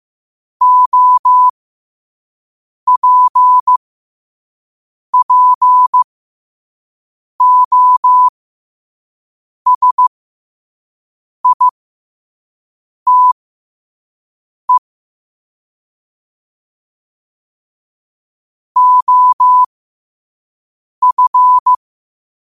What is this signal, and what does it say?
Morse code, heard as a keyed tone, about 15 wpm, spaced as for 5 wpm, 1000 Hz. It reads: OPPOSITE OF
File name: qotdfiles / 5wpm